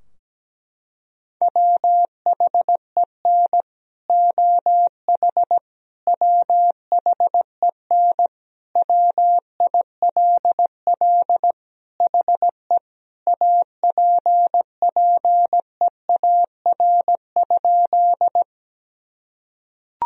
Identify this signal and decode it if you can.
Morse code, heard as a keyed tone, about 17 wpm, 700 Hz. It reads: WHEN OH WHEN WILL HE APPEAR?
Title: Sentences from Top 1000 Words - Review Part 2 - 17wpm